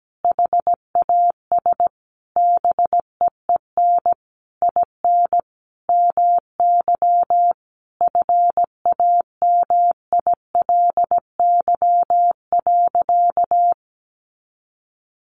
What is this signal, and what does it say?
Morse code, heard as a keyed tone, about 17 wpm, 700 Hz. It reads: HAS BEEN IN MY FAMILY.
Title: Sentences from Top 1200 Words 17wpm